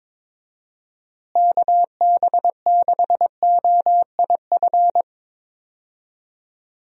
Morse code, heard as a keyed tone, about 22 wpm, 700 Hz. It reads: KB6OIF
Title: USA Call Signs - 6 Characters - Repeated 2x - 22wpm